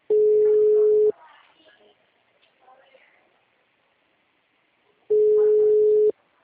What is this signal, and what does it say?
Ringback tone (Europe, 425 Hz), with speech.